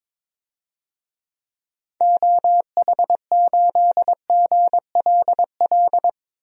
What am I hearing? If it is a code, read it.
Morse code, heard as a keyed tone, about 22 wpm, 700 Hz. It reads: OH8GLL
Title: DXCC-100 Call Signs - 6 Characters - Repeated 3x - 22wpm